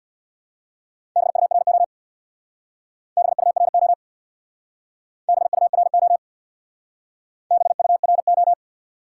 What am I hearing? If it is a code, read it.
Morse code, heard as a keyed tone, about 67 wpm, 700 Hz. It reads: BURO BURO BURO BURO